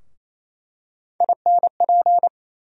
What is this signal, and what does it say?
Morse code, heard as a keyed tone, about 28 wpm, 700 Hz. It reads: INP
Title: USA Call Sign Postfixes - 3 Letters 28wpm